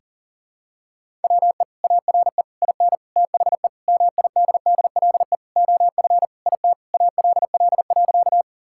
Morse code, heard as a keyed tone, about 40 wpm, 700 Hz. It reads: WE ARE IN THE MIDDLE OF IT ALL.
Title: Sentences from Top 1000 Words - Review Part 6 - 40wpm